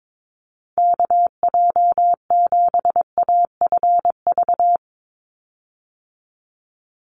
Morse code, heard as a keyed tone, about 22 wpm, 700 Hz. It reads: KJ7AFV
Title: USA Call Signs 22wpm